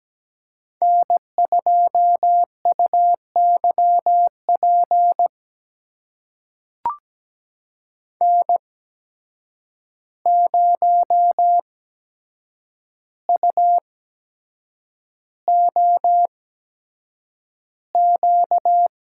Morse code, heard as a keyed tone, about 17 wpm, spaced as for 7 wpm, 700 Hz. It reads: N2UYP N0UOQ
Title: USA Call Signs - 8x Character Spacing - 17wpm